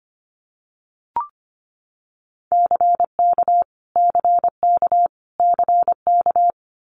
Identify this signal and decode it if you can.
Morse code, heard as a keyed tone, about 25 wpm, 700 Hz. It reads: CK CK CK
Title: Multi-letter Phonemes Repeated 3x 25wpm